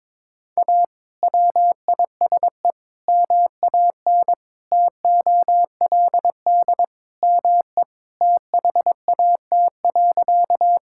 Morse code, heard as a keyed tone, about 22 wpm, 700 Hz. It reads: A WISE MAN TOLD ME THAT.
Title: Sentences from Top 1600 Words 22wpm